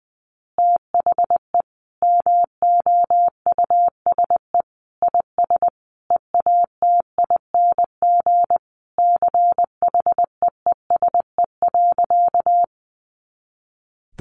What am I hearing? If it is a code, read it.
Morse code, heard as a keyed tone, about 20 wpm, 700 Hz. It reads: THE MOUSE IS EATING CHEESE.
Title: Easy Phrases 20wpm